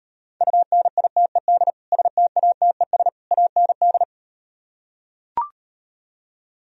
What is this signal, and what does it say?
Morse code, heard as a keyed tone, about 38 wpm, 700 Hz. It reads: UNITED STATES AND